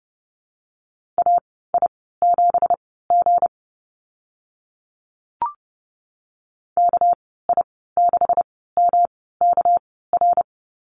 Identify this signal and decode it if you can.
Morse code, heard as a keyed tone, about 29 wpm, spaced as for 17 wpm, 700 Hz. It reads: AI7G KI6MKR
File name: Lesson 37g - Letter x - Call Signs 30wpm